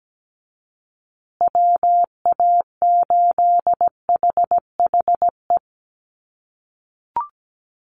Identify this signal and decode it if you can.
Morse code, heard as a keyed tone, about 17 wpm, 700 Hz. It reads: WA8HHE
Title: USA Call Signs Repeated 2x 17wpm